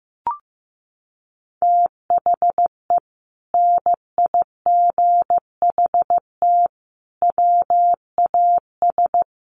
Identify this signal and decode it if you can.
Morse code, heard as a keyed tone, about 15 wpm, 700 Hz. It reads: THE NIGHT WAS
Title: Sentences from Top 1200 Words 15wpm